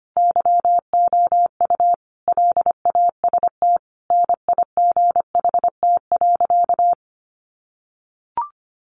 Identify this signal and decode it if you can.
Morse code, heard as a keyed tone, about 25 wpm, 700 Hz. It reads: YOU LAST NIGHT.
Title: Sentences from Top 1300 Words 25wpm